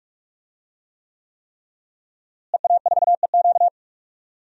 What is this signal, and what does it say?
Morse code, heard as a keyed tone, about 43 wpm, 700 Hz. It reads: EA4EQ
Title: DXCC-100 Call Signs - 5 Characters - Repeated 3x - 45wpm